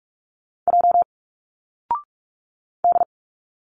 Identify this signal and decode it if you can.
Morse code, heard as a keyed tone, about 46 wpm, 700 Hz. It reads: J D
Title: Single Letters 45wpm